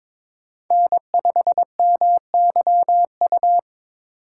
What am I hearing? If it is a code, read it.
Morse code, heard as a keyed tone, about 22 wpm, 700 Hz. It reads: N5MYU
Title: USA Call Signs - 4x Character Spacing - 22wpm